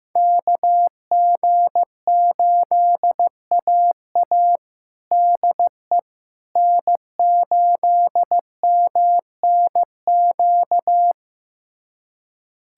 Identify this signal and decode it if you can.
Morse code, heard as a keyed tone, about 15 wpm, 700 Hz. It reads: KG8AA DE N8MNQ
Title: USA Callsign DE Callsign - 5 Characters - 15wpm